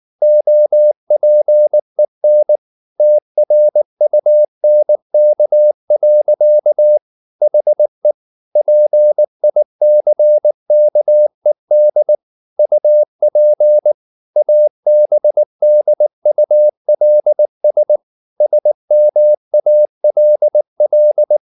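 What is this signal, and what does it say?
Morse code, heard as a keyed tone, about 19 wpm, 600 Hz. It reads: OPEN TRUNK. HE PICKED UP ABDULS SMALL
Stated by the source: Morse code at 19 wpm from CW med Gnister, reading Ken Follett's novel Never in English.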